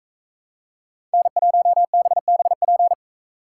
Morse code, heard as a keyed tone, about 40 wpm, 700 Hz. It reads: N1BBP